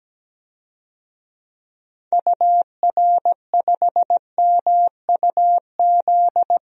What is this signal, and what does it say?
Morse code, heard as a keyed tone, about 17 wpm, 700 Hz. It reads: UR5MUZ
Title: DXCC-100 Call Signs - 6 Characters - Repeated 2x - 17wpm